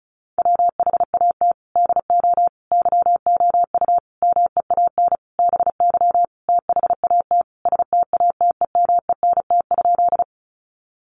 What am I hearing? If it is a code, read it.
Morse code, heard as a keyed tone, about 34 wpm, 700 Hz. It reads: WHAT DO YOU MEAN BY THAT STATEMENT?
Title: Sentences from Top 1200 Words 35wpm